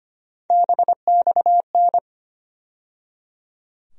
Morse code, heard as a keyed tone, about 25 wpm, 700 Hz. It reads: BXN
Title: USA Call Sign Postfixes - 3 Letters 25wpm